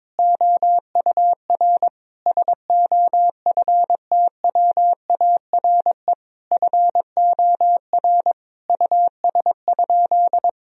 Morse code, heard as a keyed tone, about 21 wpm, 700 Hz. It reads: OUR SOFTWARE FOR US?
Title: Sentences from Top 1000 Words - Review Part 3 - 22wpm